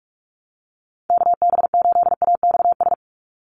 Morse code, heard as a keyed tone, about 44 wpm, 700 Hz. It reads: KB8AXS
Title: USA Call Signs 45wpm